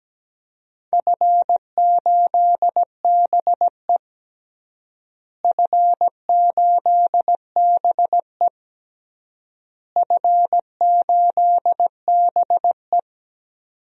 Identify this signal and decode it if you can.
Morse code, heard as a keyed tone, about 17 wpm, 700 Hz. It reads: F8BE F8BE F8BE
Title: DXCC-100 Call Signs - 4 Characters - Repeated 3x - 17wpm